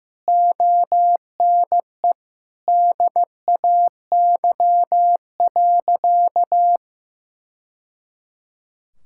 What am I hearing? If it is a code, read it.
Morse code, heard as a keyed tone, about 15 wpm, 700 Hz. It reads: ONE DAY.
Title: Sentences from Top 1000 Words - Review Part 6 - 15wpm